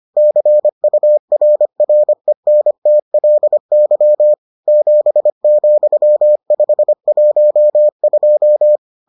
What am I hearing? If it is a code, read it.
Morse code, heard as a keyed tone, about 25 wpm, 600 Hz. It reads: CURRENTLY 7,512